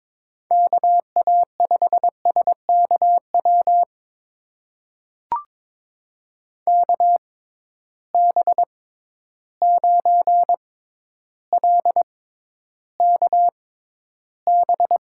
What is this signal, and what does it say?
Morse code, heard as a keyed tone, about 22 wpm, spaced as for 11 wpm, 700 Hz. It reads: KA5SKW KB9LKB
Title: USA Call Signs - 6 Characters - 6x Character Spacing - 22wpm